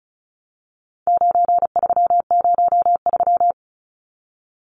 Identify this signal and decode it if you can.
Morse code, heard as a keyed tone, about 34 wpm, 700 Hz. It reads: 9303
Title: Numbers - 4-Digit 35wpm